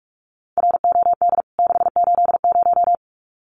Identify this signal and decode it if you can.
Morse code, heard as a keyed tone, about 44 wpm, 700 Hz. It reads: ROD 680
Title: CWT Exchange 45wpm